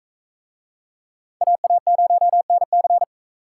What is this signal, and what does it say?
Morse code, heard as a keyed tone, about 40 wpm, 700 Hz. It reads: AA0NC